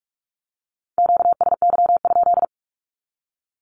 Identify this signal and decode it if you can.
Morse code, heard as a keyed tone, about 46 wpm, 700 Hz. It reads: QSY?
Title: Q-Codes with Spoken Meaning - 45wpm